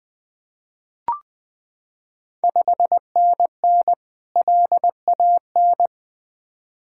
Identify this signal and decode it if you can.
Morse code, heard as a keyed tone, about 20 wpm, 700 Hz. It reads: 5NN LAN
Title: Ontario QSO Party Exchange 20wpm